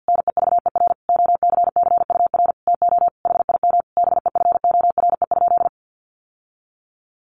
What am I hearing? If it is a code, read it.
Morse code, heard as a keyed tone, about 50 wpm, 700 Hz. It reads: NEVER OCCUR TO HIM BEFORE?
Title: Sentences from Top 900 Words 50wpm